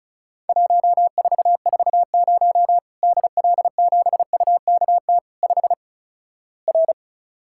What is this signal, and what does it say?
Morse code, heard as a keyed tone, about 36 wpm, 700 Hz. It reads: 1440 DL7UKT 5 R
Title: Worked All Europe DX Contest - QTC Exchange - 35wpm